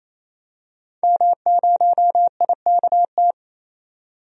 Morse code, heard as a keyed tone, about 28 wpm, 700 Hz. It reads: M0IKT
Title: DXCC-100 Call Signs - 5 Characters - 28wpm